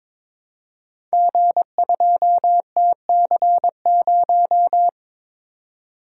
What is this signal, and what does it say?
Morse code, heard as a keyed tone, about 22 wpm, 690 Hz. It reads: G2TC0
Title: Code Groups - 5 Letter:Numbers - 22wpm